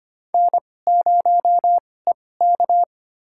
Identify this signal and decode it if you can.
Morse code, heard as a keyed tone, about 25 wpm, spaced as for 18 wpm, 700 Hz. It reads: N0EK
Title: USA Call Signs - 2x Character Spacing - 25wpm